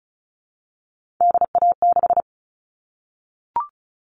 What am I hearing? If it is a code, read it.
Morse code, heard as a keyed tone, about 34 wpm, 700 Hz. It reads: DA6
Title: All Cut Numbers - 3-Digit 35wpm